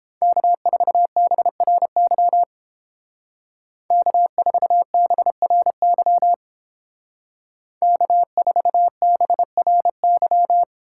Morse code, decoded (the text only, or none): K4BRY K4BRY K4BRY